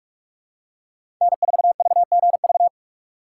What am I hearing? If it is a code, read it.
Morse code, heard as a keyed tone, about 47 wpm, 700 Hz. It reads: N4VGV
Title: USA Call Signs - 4x Character Spacing - 45wpm